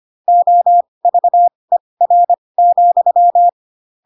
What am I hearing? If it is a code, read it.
Morse code, heard as a keyed tone, about 25 wpm, spaced as for 20 wpm, 700 Hz. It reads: OVER,
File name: Pinocchio / Pinocchio ch2 15 25 ik1hge